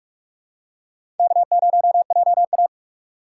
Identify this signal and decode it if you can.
Morse code, heard as a keyed tone, about 48 wpm, 700 Hz. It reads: K0JA
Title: USA Call Signs - 4 Characters - 2x Character Spacing - 45wpm